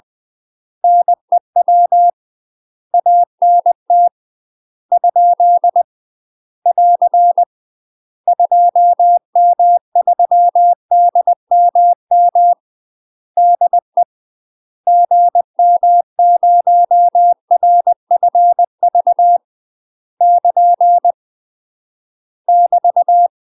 Morse code, heard as a keyed tone, about 20 wpm, 700 Hz. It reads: NEW ANT ? <AR> 2M3DMM DE GM0RFV <KN> <BT>
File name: UK MASTER QSOs TRAINING/20WPM